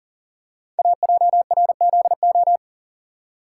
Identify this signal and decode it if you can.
Morse code, heard as a keyed tone, about 41 wpm, 690 Hz. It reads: AJRZO
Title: Code Groups - 5 Letter:Numbers - 40wpm